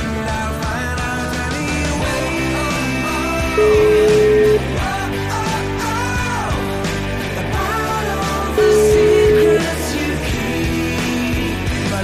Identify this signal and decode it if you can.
Ringback tone (Europe, 425 Hz), with noise.